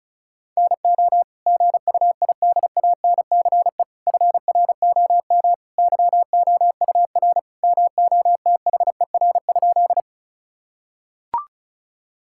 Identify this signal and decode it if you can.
Morse code, heard as a keyed tone, about 34 wpm, 700 Hz. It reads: NO GUIDANCE FROM YOUR MOTHER?
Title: Sentences from Top 1600 Words 35wpm